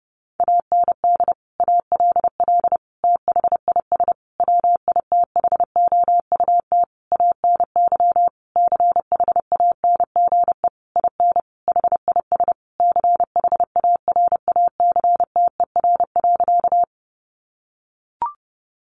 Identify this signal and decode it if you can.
Morse code, heard as a keyed tone, about 28 wpm, 700 Hz. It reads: AND ALL THIS WITHOUT ANY CHANGE IN HIS CHARACTER.
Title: Sentences from Top 1200 Words 30wpm